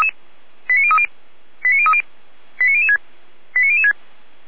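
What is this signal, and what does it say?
Five-tone selective call (ZVEI), heard as quick runs of five tones, tones of about 70 ms. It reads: 89030 89030 89006 89006
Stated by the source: ZVEI five-tone selective call received by SDR.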